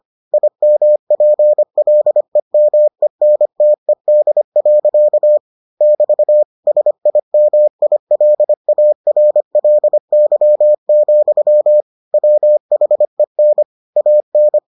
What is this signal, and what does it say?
Morse code, heard as a keyed tone, about 25 wpm, 600 Hz. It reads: IMPLEMENTED. <BT> SIMILARLY, WHEN AN